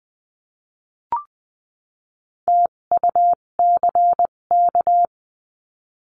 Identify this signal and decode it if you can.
Morse code, heard as a keyed tone, about 20 wpm, spaced as for 17 wpm, 700 Hz. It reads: TUCK
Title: 257 Lesson 42b - Words 14-20wpm